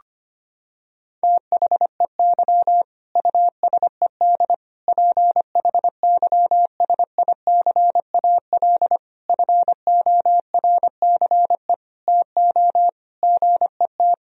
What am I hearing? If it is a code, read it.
Morse code, heard as a keyed tone, about 25 wpm, 700 Hz. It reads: THEY USED PHYSICAL FORCE TO GET
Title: Sentences from Top 1000 Words - Review Part 4 - 25wpm